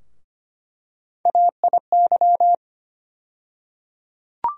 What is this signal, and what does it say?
Morse code, heard as a keyed tone, about 25 wpm, 700 Hz. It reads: AIY
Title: USA Call Sign Postfixes - 3 Letters 25wpm